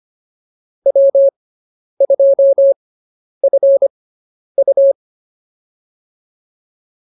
Morse code, heard as a keyed tone, about 25 wpm, spaced as for 10 wpm, 550 Hz. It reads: W2FU